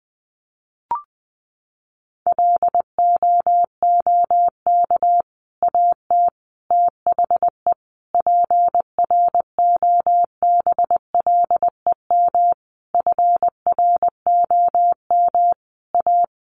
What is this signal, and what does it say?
Morse code, heard as a keyed tone, about 19 wpm, 700 Hz. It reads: LOOK AT THE PROBLEM FROM A
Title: Sentences from Top 1500 Words 20wpm